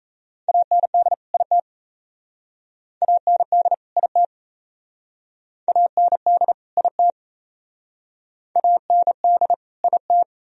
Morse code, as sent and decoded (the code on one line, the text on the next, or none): .- -. -.. / .. - / .- -. -.. / .. - / .- -. -.. / .. - / .- -. -.. / .. -
AND IT AND IT AND IT AND IT